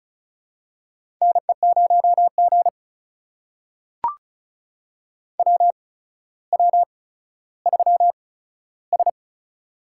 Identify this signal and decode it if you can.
Morse code, heard as a keyed tone, about 36 wpm, spaced as for 15 wpm, 700 Hz. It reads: NE0G WW3S
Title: USA Call Signs - 4 Characters - 8x Character Spacing - 35wpm